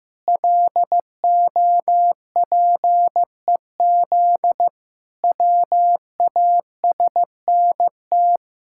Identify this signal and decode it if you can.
Morse code, heard as a keyed tone, about 15 wpm, 700 Hz. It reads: LOPEZ WASNT
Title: Pangram Challenge v2 - 15wpm